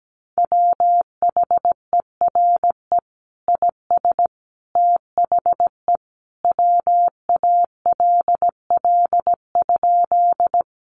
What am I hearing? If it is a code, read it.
Morse code, heard as a keyed tone, about 17 wpm, 700 Hz. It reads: WHERE IS THE WALL?
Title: Sentences from Top 1100 Words 17wpm